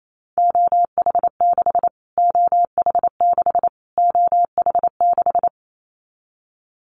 Morse code, heard as a keyed tone, about 28 wpm, 700 Hz. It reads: OH6 OH6 OH6
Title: Callsign Prefixes for Top 25 DXCC - Repeated 3x - 28wpm